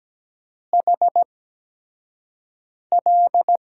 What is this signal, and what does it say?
Morse code, heard as a keyed tone, about 17 wpm, spaced as for 5 wpm, 700 Hz. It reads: HL
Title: USA Call Signs - 8x Character Spacing - 17wpm